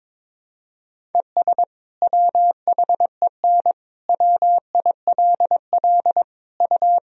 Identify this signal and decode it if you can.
Morse code, heard as a keyed tone, about 22 wpm, 700 Hz. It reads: ES WHEN WILL U
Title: Sentences from Top 100 Words and Abbreviations 22wpm